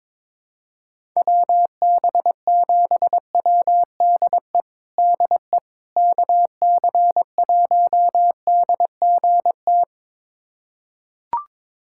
Morse code, heard as a keyed tone, about 22 wpm, 700 Hz. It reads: WB7WDE DE KC1DGT
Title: USA Callsign DE Callsign - 6 Characters - 22wpm